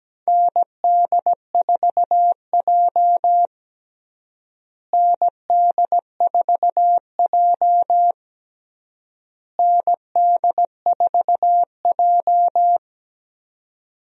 Morse code, heard as a keyed tone, about 17 wpm, 700 Hz. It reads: ND4J ND4J ND4J
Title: USA Call Signs - 4 Characters - Repeated 3x - 17wpm